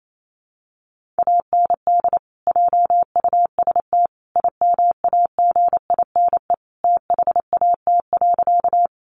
Morse code, heard as a keyed tone, about 28 wpm, 700 Hz. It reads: AND JUST IMAGINE THAT.
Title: Sentences from Top 1400 Words 28wpm